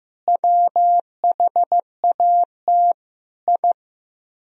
Morse code, heard as a keyed tone, about 15 wpm, 700 Hz. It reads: WHAT I
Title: Speed-Racing - Sets of 4 Words 15wpm